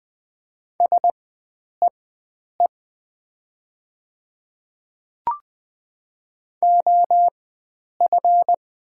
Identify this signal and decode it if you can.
Morse code, heard as a keyed tone, about 20 wpm, spaced as for 9 wpm, 700 Hz. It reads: SEE OF
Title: 4x Character Spacing - Top 100 Words - 20wpm